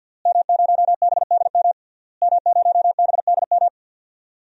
Morse code, heard as a keyed tone, about 52 wpm, 700 Hz. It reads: M0BDM M0BDM
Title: DXCC-100 Call Signs - 5 Characters - Repeated 3x - 50wpm